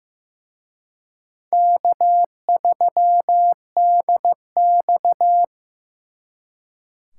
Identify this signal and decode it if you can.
Morse code, heard as a keyed tone, about 18 wpm, 700 Hz. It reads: K3DX
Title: Speed Racing - USA 1x2 Call Signs 15wpm